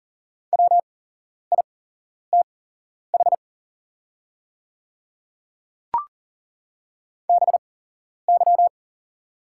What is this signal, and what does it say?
Morse code, heard as a keyed tone, about 40 wpm, spaced as for 11 wpm, 690 Hz. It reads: WITH BY
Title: 8x Character Spacing - Top 200 Words - 40wpm